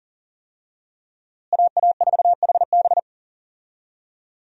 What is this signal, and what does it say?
Morse code, heard as a keyed tone, about 40 wpm, 700 Hz. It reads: AA4HB
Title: USA Call Signs - 5 Characters - Repeated 2x - 40wpm